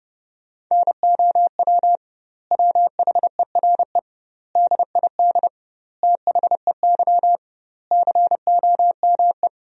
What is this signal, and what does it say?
Morse code, heard as a keyed tone, about 30 wpm, 700 Hz. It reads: NOW WHERE DID THEY COME
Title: 2x Word-Spacing - Sentences from Top 200 Words - 30wpm